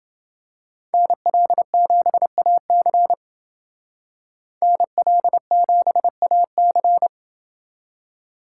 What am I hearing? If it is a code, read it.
Morse code, heard as a keyed tone, about 28 wpm, 700 Hz. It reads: NL7AC NL7AC